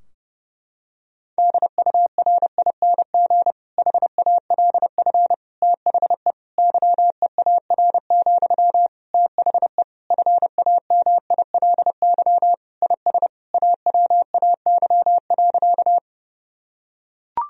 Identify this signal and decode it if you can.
Morse code, heard as a keyed tone, about 30 wpm, 700 Hz. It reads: DURING HALF THE YEAR, THE FAMILY IS AWAY.
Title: Sentences from Top 1000 Words - Review Part 2 - 30wpm